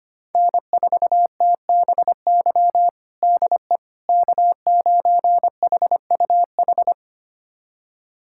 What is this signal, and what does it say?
Morse code, heard as a keyed tone, about 25 wpm, 700 Hz. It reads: N4TBY DE K9HUH
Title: USA Callsign DE Callsign - 5 Characters - 25wpm